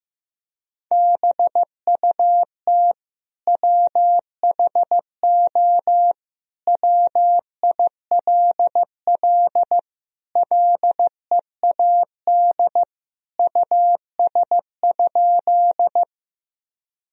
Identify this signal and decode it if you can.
Morse code, heard as a keyed tone, about 15 wpm, 700 Hz. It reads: BUT WHO WILL LEAD US?
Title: Sentences from Top 1000 Words - Review Part 2 - 15wpm